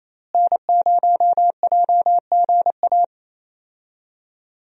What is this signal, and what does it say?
Morse code, heard as a keyed tone, about 28 wpm, 700 Hz. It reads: N0JGA
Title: USA Call Signs - 5 Characters - Repeated 3x - 28wpm